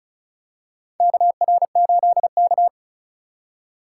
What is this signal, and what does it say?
Morse code, heard as a keyed tone, about 36 wpm, 700 Hz. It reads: KR8K
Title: USA Call Signs - 4 Characters - 3x Character Spacing - 35wpm